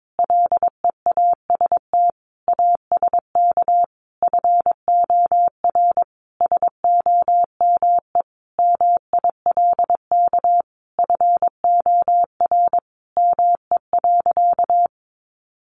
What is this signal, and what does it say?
Morse code, heard as a keyed tone, about 22 wpm, 700 Hz. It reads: LEAST ASK FOR SOME MILK FOR ME.
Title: Sentences from Top 1500 Words 22wpm